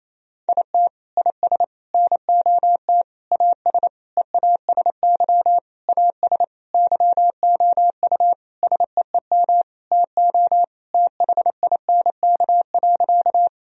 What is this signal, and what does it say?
Morse code, heard as a keyed tone, about 28 wpm, 700 Hz. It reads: IT IS NOT AS EASY AS YOU SEEM TO THINK.
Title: Sentences from Top 1000 Words - Review Part 1 - 28wpm